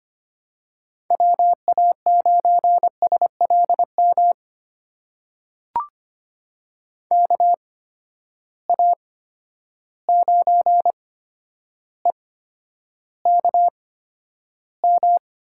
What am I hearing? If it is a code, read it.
Morse code, heard as a keyed tone, about 25 wpm, spaced as for 11 wpm, 700 Hz. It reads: WA9SLM KA9EKM
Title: USA Call Signs - 6 Characters - 8x Character Spacing - 25wpm